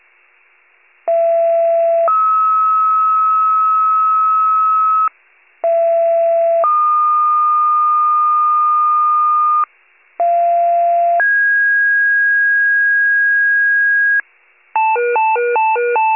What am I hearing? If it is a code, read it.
Two-tone page, heard as a one-second tone, then a three-second tone, about 670 then 1290 Hz, about 670 then 1220 Hz, about 690 then 1650 Hz, with noise. It is heard from 1.1 to 14.2 s.